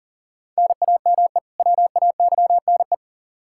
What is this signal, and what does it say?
Morse code, heard as a keyed tone, about 40 wpm, 700 Hz. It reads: NAME WAYNE
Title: Practice QSO - Exchange 1 - Beginning - 40wpm